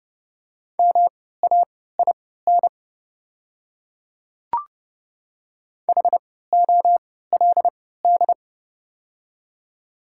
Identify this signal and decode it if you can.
Morse code, heard as a keyed tone, about 30 wpm, spaced as for 17 wpm, 700 Hz. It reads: MAIN HOLD
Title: 3x Character Spacing - Top 500 Words - 30wpm